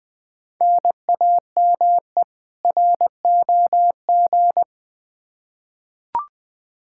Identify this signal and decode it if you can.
Morse code, heard as a keyed tone, about 20 wpm, 700 Hz. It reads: NAME ROG
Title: Practice QSO - Exchange 1 - Beginning - 20wpm